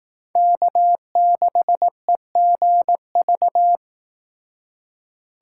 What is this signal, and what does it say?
Morse code, heard as a keyed tone, about 18 wpm, 700 Hz. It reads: K6EGV
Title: Speed Racing - USA Call Signs 15wpm